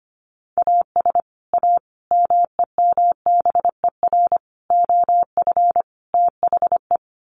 Morse code, heard as a keyed tone, about 25 wpm, 700 Hz. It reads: AS A MEMBER OF THE
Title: Sets of 5 Words 25wpm